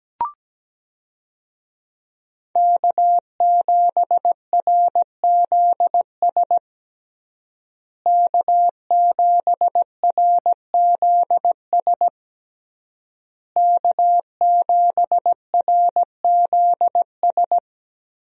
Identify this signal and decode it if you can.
Morse code, heard as a keyed tone, about 17 wpm, 700 Hz. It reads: K7RZS K7RZS K7RZS
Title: USA Call Signs - 5 Characters - Repeated 3x - 17wpm